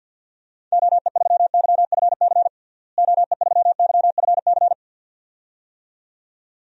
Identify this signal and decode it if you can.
Morse code, heard as a keyed tone, about 53 wpm, 700 Hz. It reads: OE3YFC OE3YFC
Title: DXCC-100 Call Signs - 6 Characters - Repeated 2x - 50wpm